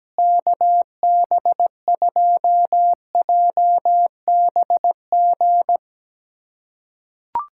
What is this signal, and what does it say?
Morse code, heard as a keyed tone, about 17 wpm, 700 Hz. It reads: KB2JBG
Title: USA Callsign DE Callsign - 6 Characters - 17wpm